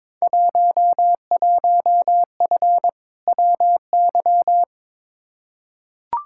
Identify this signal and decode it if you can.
Morse code, heard as a keyed tone, about 22 wpm, 700 Hz. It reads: 11F WY
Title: Field Day Exchanges Encore - 22wpm